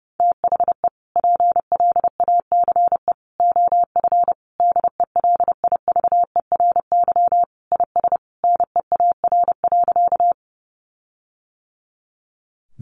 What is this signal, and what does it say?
Morse code, heard as a keyed tone, about 28 wpm, 700 Hz. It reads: THE PLACE OF DELIVERY IS NEAR.
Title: Sentences from Top 1500 Words 30wpm